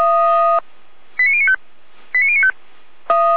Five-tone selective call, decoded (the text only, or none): 89095 89095